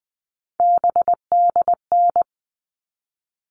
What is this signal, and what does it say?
Morse code, heard as a keyed tone, about 20 wpm, 700 Hz. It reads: BDN
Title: All Cut Numbers - 3-Digit 20wpm